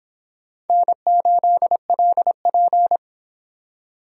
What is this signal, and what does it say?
Morse code, heard as a keyed tone, about 23 wpm, 700 Hz. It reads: N8LP